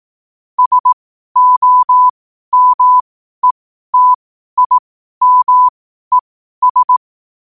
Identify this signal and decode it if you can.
Morse code, heard as a keyed tone, about 18 wpm, spaced as for 12 wpm, 1000 Hz. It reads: SOMETIMES
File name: qotdfiles / 12wpm